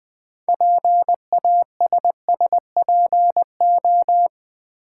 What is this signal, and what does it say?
Morse code, heard as a keyed tone, about 20 wpm, 700 Hz. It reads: PASSPO
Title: Build A Word - 2 Letters At A Time - 8-Letter Words - 20wpm